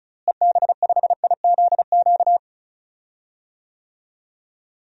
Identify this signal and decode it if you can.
Morse code, heard as a keyed tone, about 34 wpm, 700 Hz. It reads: EB5IZQ
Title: DXCC-100 Call Signs - 6 Characters - Repeated 3x - 35wpm